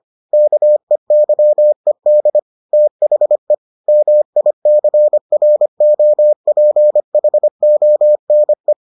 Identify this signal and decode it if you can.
Morse code, heard as a keyed tone, about 25 wpm, 600 Hz. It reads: KEYED THE MICROPHONE